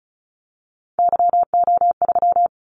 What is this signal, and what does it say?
Morse code, heard as a keyed tone, about 34 wpm, 700 Hz. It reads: YO3
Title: Callsign Prefixes for Top 25 DXCC - 35wpm